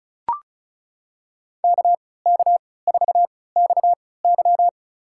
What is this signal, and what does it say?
Morse code, heard as a keyed tone, about 36 wpm, spaced as for 20 wpm, 700 Hz. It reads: KK4XY
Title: USA Call Signs - 5 Characters - 3x Character Spacing - 35wpm